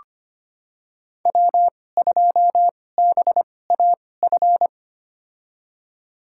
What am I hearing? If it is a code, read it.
Morse code, heard as a keyed tone, about 25 wpm, spaced as for 18 wpm, 700 Hz. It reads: W2BAF
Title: USA Call Signs - 5 Characters - 2x Character Spacing - 25wpm